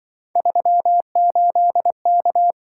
Morse code, heard as a keyed tone, about 24 wpm, 700 Hz. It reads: 38K
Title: Speed-Racing - Three Letter-Number Combinations - 20wpm